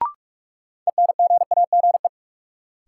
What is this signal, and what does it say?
Morse code, heard as a keyed tone, about 42 wpm, 700 Hz. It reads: ENGAGE
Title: Sentences from Top 1500 Words - New Word Drill Alt - 45wpm